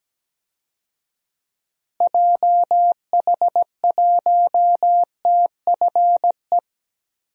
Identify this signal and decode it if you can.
Morse code, heard as a keyed tone, about 17 wpm, 700 Hz. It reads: JH1TFE
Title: DXCC-100 Call Signs - 6 Characters - Repeated 2x - 17wpm